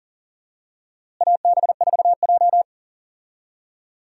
Morse code, heard as a keyed tone, about 37 wpm, 700 Hz. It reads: AB4J
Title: USA Call Signs - 4 Characters - Repeated 2x - 40wpm